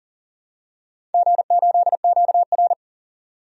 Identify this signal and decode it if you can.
Morse code, heard as a keyed tone, about 40 wpm, 700 Hz. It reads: G8QR
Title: DXCC-100 Call Signs - 4 Characters - 40wpm